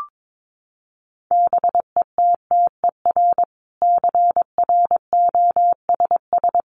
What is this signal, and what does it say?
Morse code, heard as a keyed tone, about 22 wpm, 700 Hz. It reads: BETTER CROSS
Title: Sentences from Top 400 Words 22wpm